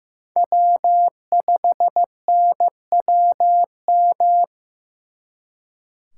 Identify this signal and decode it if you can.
Morse code, heard as a keyed tone, about 19 wpm, 700 Hz. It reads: W5NWM